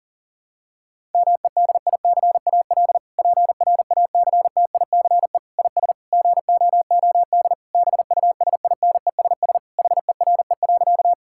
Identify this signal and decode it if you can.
Morse code, heard as a keyed tone, about 40 wpm, 700 Hz. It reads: MEDICAL PRACTICE IS GOOD BUSINESS HERE.
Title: Sentences from Top 1000 Words - Review Part 5 - 40wpm